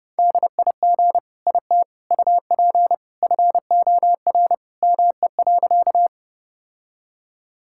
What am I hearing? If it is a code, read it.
Morse code, heard as a keyed tone, about 32 wpm, 700 Hz. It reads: DIG IT UP FOR ME.
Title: Sentences from Top 1600 Words 30wpm